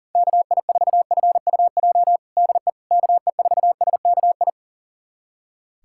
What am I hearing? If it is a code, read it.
Morse code, heard as a keyed tone, about 40 wpm, 700 Hz. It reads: KI4FUJ DE KE4SKI
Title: USA Callsign DE Callsign - 6 Characters - 40wpm